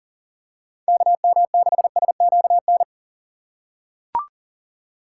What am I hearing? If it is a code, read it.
Morse code, heard as a keyed tone, about 40 wpm, 700 Hz. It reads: KM6SQN
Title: USA Call Signs - 6 Characters - 2x Character Spacing - 40wpm